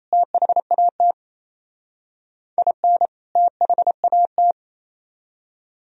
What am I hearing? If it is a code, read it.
Morse code, heard as a keyed tone, about 34 wpm, 700 Hz. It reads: THAT IN THAT
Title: Speed-Racing - Sets of 2 Words 28wpm